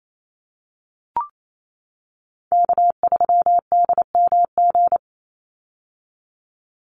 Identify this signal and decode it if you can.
Morse code, heard as a keyed tone, about 28 wpm, 700 Hz. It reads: K3DMG
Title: Call Signs - 28wpm